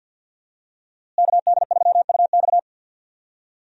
Morse code, heard as a keyed tone, about 52 wpm, 700 Hz. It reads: KD3UX
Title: USA Call Signs - 5 Characters - Repeated 3x - 50wpm